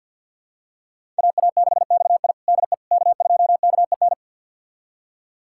Morse code, heard as a keyed tone, about 48 wpm, 700 Hz. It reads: AA6XI DE K2KEN
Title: USA Callsign DE Callsign - 5 Characters - 50wpm